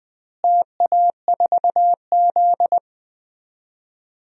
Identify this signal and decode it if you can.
Morse code, heard as a keyed tone, about 20 wpm, 700 Hz. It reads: TA4Z
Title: DXCC-100 Call Signs - 4 Characters - Repeated 2x - 20wpm